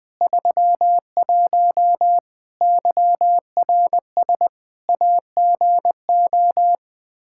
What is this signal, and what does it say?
Morse code, heard as a keyed tone, about 21 wpm, 700 Hz. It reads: 31 YRS AGO
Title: Practice QSO - Exchange 3 - More Info - 20wpm